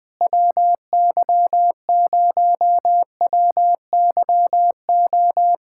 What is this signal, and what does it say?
Morse code, heard as a keyed tone, about 20 wpm, 700 Hz. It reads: WY0WYO
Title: SOTA QSO Practice 20wpm